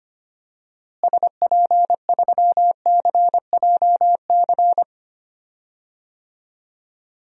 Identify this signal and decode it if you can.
Morse code, heard as a keyed tone, about 25 wpm, 700 Hz. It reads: SP3CJC
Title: DXCC-100 Call Signs - 6 Characters - Repeated 3x - 25wpm